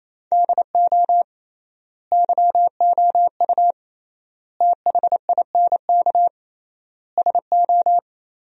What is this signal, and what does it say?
Morse code, heard as a keyed tone, about 27 wpm, 700 Hz. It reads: DO YOU THINK SO
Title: 3x Word-Spacing - Sentences from Top 200 Words - 28wpm